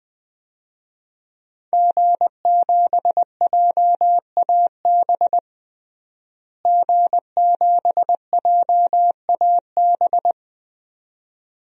Morse code, heard as a keyed tone, about 21 wpm, 700 Hz. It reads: G7JAB G7JAB
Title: DXCC-100 Call Signs - 5 Characters - Repeated 2x - 20wpm